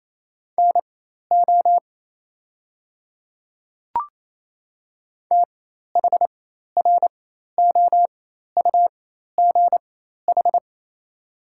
Morse code, heard as a keyed tone, about 27 wpm, spaced as for 13 wpm, 700 Hz. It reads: NO THROUGH
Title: 4x Character Spacing - Top 300 Words - 28wpm